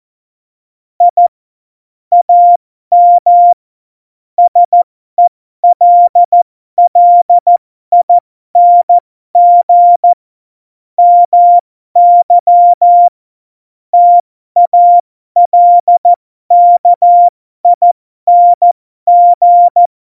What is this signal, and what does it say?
Morse code, heard as a keyed tone, about 14 wpm, spaced as for 12 wpm, 700 Hz. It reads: I AM SELLING MY TALKING
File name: Morsekurs / Witze